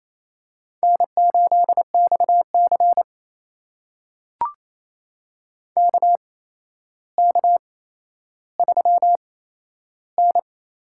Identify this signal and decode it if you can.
Morse code, heard as a keyed tone, about 28 wpm, spaced as for 12 wpm, 700 Hz. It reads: N8XC KK3N
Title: USA Call Signs - 4 Characters - 8x Character Spacing - 28wpm